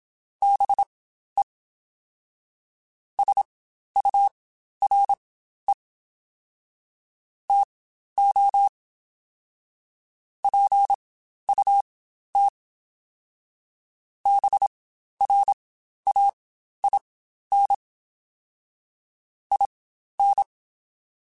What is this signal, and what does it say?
Morse code, heard as a keyed tone, about 27 wpm, spaced as for 12 wpm, 800 Hz. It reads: BE SURE TO PUT BRAIN IN
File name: Spaced / Chuckles Spaced-mp3s / BrainMouth sp @25